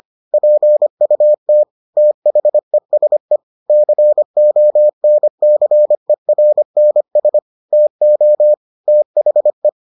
Morse code, heard as a keyed tone, about 25 wpm, 600 Hz. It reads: PUT THESE CONCERNS TO THE